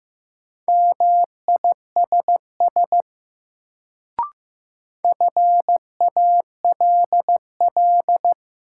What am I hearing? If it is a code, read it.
Morse code, heard as a keyed tone, about 16 wpm, 700 Hz. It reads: MISS FALL
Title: Sentences from Top 500 Words - New Word Drill - 15wpm